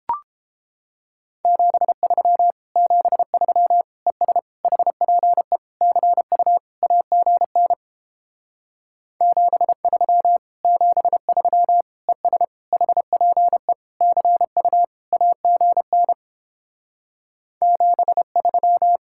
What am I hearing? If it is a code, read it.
Morse code, heard as a keyed tone, about 29 wpm, 700 Hz. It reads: 73 73 ES HPE CU AGN 73 73 ES HPE CU AGN 73